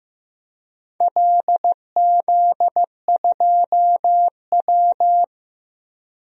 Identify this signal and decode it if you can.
Morse code, heard as a keyed tone, about 16 wpm, 700 Hz. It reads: LZ2W
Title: DXCC-100 Call Signs - 4 Characters - Repeated 2x - 15wpm